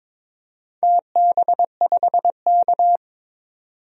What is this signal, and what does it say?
Morse code, heard as a keyed tone, about 22 wpm, 700 Hz. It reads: TB5K
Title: DXCC-100 Call Signs - 4 Characters - 22wpm